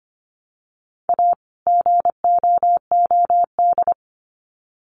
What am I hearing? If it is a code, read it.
Morse code, heard as a keyed tone, about 25 wpm, 700 Hz. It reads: A GOOD
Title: Sets of 2 Words 25wpm